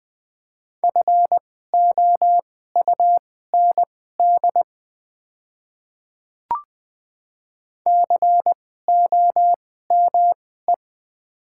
Morse code, heard as a keyed tone, about 20 wpm, spaced as for 14 wpm, 700 Hz. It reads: FOUND COME
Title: 2x Character Spacing - Top 300 Words - 20wpm